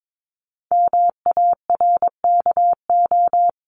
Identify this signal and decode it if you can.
Morse code, heard as a keyed tone, about 22 wpm, 700 Hz. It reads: MARKO
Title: CWT Exchange 22wpm